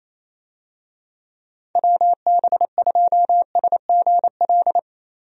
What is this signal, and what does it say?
Morse code, heard as a keyed tone, about 28 wpm, 700 Hz. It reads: WB2SGL
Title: USA Call Signs - 6 Characters - 28wpm